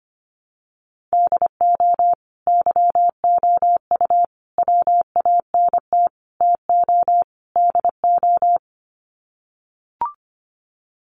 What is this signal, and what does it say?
Morse code, heard as a keyed tone, about 25 wpm, 700 Hz. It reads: DO YOU WANT TO DO
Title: Sets of 5 Words Encore 25wpm